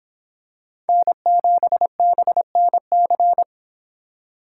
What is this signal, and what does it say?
Morse code, heard as a keyed tone, about 26 wpm, 700 Hz. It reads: N7BNC